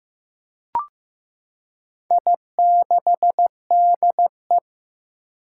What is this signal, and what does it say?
Morse code, heard as a keyed tone, about 15 wpm, 700 Hz. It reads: I6DE
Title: Four Challenging Letter-Number Combinations 15wpm